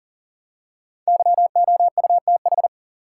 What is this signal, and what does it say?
Morse code, heard as a keyed tone, about 40 wpm, 700 Hz. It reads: YOUTH
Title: Five Letter Words - 40wpm